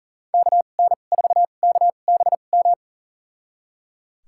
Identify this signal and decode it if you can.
Morse code, heard as a keyed tone, about 37 wpm, spaced as for 28 wpm, 700 Hz. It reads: KN4KBM
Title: USA Call Signs - 6 Characters - 2x Character Spacing - 40wpm